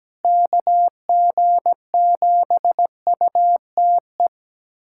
Morse code, heard as a keyed tone, about 17 wpm, 700 Hz. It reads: KG7UTE
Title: USA Callsign DE Callsign - 6 Characters - 17wpm